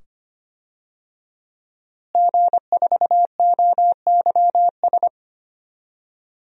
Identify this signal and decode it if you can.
Morse code, heard as a keyed tone, about 25 wpm, 700 Hz. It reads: G4OYS
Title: DXCC-100 Call Signs - 5 Characters - Repeated 2x - 25wpm